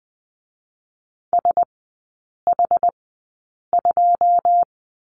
Morse code, heard as a keyed tone, about 20 wpm, spaced as for 8 wpm, 700 Hz. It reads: SH2
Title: 138 Lesson 23e - Letter f - Three Characters 20wpm